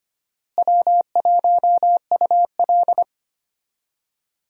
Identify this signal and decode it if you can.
Morse code, heard as a keyed tone, about 25 wpm, 700 Hz. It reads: W1UL
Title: USA Call Signs - 4 Characters - Repeated 2x - 25wpm